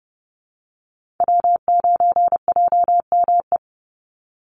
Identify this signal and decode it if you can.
Morse code, heard as a keyed tone, about 30 wpm, 700 Hz. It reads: W9JME
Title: USA Call Signs 30wpm